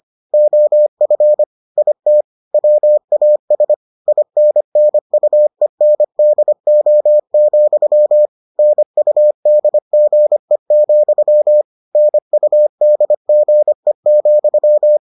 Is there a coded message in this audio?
Yes: Morse code from 0.3 to 15.0 s, heard as a keyed tone, about 25 wpm, 600 Hz. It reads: OF IT WAS INNUENDO, NUDGE, NUDGE,